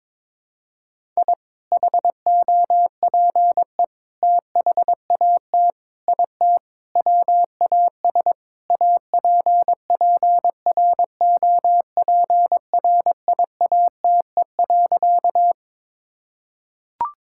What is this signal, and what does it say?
Morse code, heard as a keyed tone, about 21 wpm, 700 Hz. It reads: I HOPE THAT IT WAS APPROPRIATE.
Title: Sentences from Top 1000 Words - Review Part 5 - 22wpm